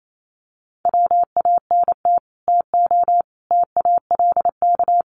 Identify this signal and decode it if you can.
Morse code, heard as a keyed tone, about 28 wpm, 700 Hz. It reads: WANT TO TALK
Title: Sets of 4 Words 28wpm